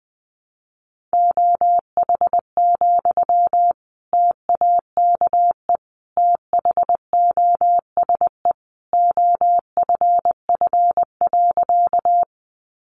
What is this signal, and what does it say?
Morse code, heard as a keyed tone, about 20 wpm, 700 Hz. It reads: OH, TAKE THOSE OFF.
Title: Sentences from Top 500 Words 20wpm